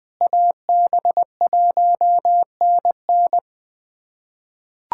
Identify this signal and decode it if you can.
Morse code, heard as a keyed tone, about 20 wpm, 700 Hz. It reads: AB1NN
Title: USA Callsign DE Callsign - 5 Characters - 20wpm